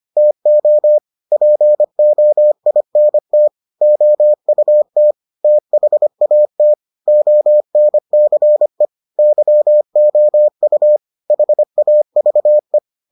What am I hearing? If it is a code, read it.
Morse code, heard as a keyed tone, about 25 wpm, 600 Hz. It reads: TO POINT OUT THAT ONCE YOU HAVE